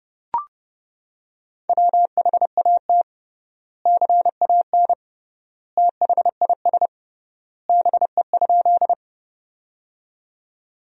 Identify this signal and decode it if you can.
Morse code, heard as a keyed tone, about 31 wpm, 700 Hz. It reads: WHAT CAN THIS BE?
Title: 3x Word-Spacing - Sentences from Top 100 Words - 30wpm